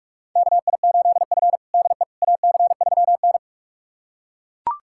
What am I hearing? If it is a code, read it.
Morse code, heard as a keyed tone, about 42 wpm, 700 Hz. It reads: KI8F DE AC3N
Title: USA Callsign DE Callsign - 4 Characters - 45wpm